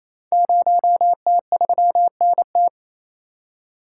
Morse code, heard as a keyed tone, about 28 wpm, 700 Hz. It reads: 0T3NT
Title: Code Groups - 5 Letter:Numbers - 28wpm